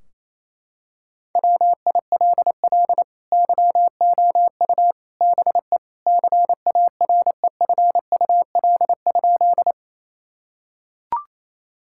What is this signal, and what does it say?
Morse code, heard as a keyed tone, about 28 wpm, 700 Hz. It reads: WILL YOU BE CAREFUL?
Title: Sentences from Top 1600 Words 28wpm